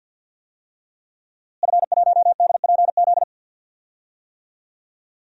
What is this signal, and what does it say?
Morse code, heard as a keyed tone, about 49 wpm, 700 Hz. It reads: F1DPZ
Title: DXCC-100 Call Signs - 5 Characters - Repeated 3x - 50wpm